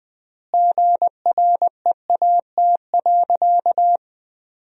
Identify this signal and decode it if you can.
Morse code, heard as a keyed tone, about 20 wpm, 700 Hz. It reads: GREAT.
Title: 3x Word-Spacing - Sentences from Top 200 Words - 20wpm